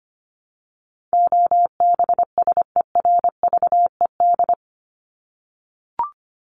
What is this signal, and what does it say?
Morse code, heard as a keyed tone, about 25 wpm, 700 Hz. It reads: OBSERVED
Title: ED Suffix - Top 100 Words with Base Words 25wpm